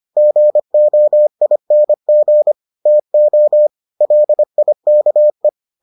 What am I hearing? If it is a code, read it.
Morse code, heard as a keyed tone, about 25 wpm, 600 Hz. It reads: GOING TO LIKE